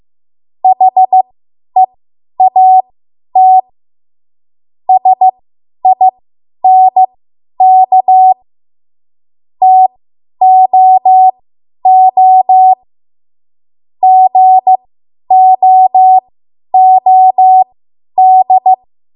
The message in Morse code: HEAT SINK TOO GOOD